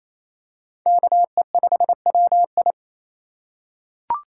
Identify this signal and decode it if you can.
Morse code, heard as a keyed tone, about 28 wpm, 700 Hz. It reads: KE5WI
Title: USA Call Signs - 5 Characters - 2x Character Spacing - 28wpm